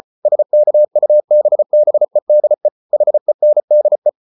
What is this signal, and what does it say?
Morse code, heard as a keyed tone, about 34 wpm, 600 Hz. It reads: SKUBBEDE HENDE
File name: Ferie QRQ 34wpm